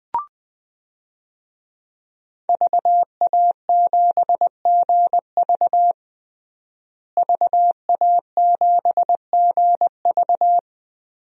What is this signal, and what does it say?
Morse code, heard as a keyed tone, about 20 wpm, 700 Hz. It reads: VA7GV VA7GV
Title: DXCC-100 Call Signs - 5 Characters - Repeated 2x - 20wpm